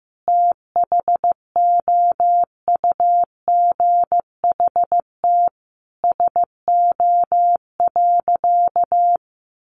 Morse code, heard as a keyed tone, about 15 wpm, 700 Hz. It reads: THOUGHT SO.
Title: Sentences from Top 1400 Words 15wpm